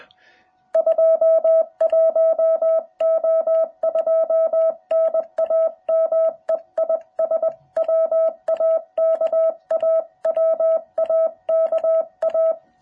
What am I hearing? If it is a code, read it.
Morse code, heard as a keyed tone, about 20 wpm, 660 Hz. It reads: 21O2 NAME IS WAKA WAKA